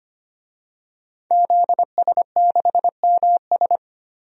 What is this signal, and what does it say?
Morse code, heard as a keyed tone, about 25 wpm, 700 Hz. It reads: ZS6MS
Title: DXCC-100 Call Signs - 5 Characters - 25wpm